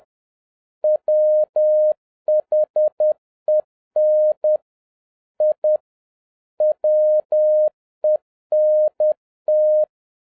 Morse code, heard as a keyed tone, about 10 wpm, 610 Hz. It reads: WHEN I WENT